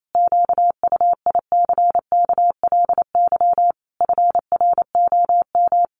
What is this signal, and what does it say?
Morse code, heard as a keyed tone, about 28 wpm, 700 Hz. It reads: QUICKLY FROM
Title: Sentences from Top 1300 Words 28wpm